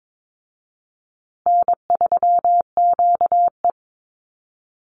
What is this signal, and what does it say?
Morse code, heard as a keyed tone, about 22 wpm, 700 Hz. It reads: N3QE
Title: Easy Call Signs Repeated 3x - 22wpm